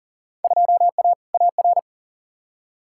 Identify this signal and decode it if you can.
Morse code, heard as a keyed tone, about 40 wpm, 700 Hz. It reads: 2A AR
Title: Field Day Exchanges - 40wpm